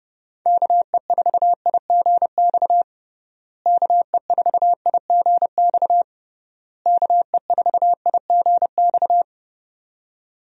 Morse code, heard as a keyed tone, about 30 wpm, 700 Hz. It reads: KE4IGX KE4IGX KE4IGX
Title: USA Call Signs - 6 Characters - Repeated 3x - 30wpm